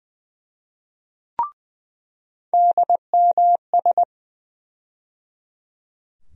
Morse code, heard as a keyed tone, about 20 wpm, 700 Hz. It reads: DMS
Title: USA Call Sign Postfixes - 3 Letters 20wpm